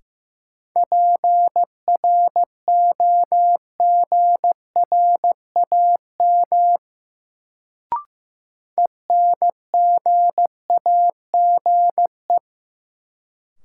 Morse code, heard as a keyed tone, about 15 wpm, 700 Hz. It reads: PROGRAM ENGAGE
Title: Sentences from Top 1500 Words - New Word Drill Alt - 15wpm